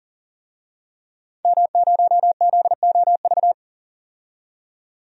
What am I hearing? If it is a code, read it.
Morse code, heard as a keyed tone, about 40 wpm, 700 Hz. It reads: M0ZOV
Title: DXCC-100 Call Signs - 5 Characters - 40wpm